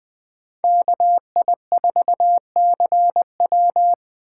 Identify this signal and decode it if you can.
Morse code, heard as a keyed tone, about 20 wpm, 700 Hz. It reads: KI4CW